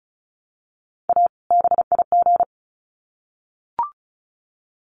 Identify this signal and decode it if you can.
Morse code, heard as a keyed tone, about 34 wpm, 700 Hz. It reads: A BIG
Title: Sets of 2 Words 35wpm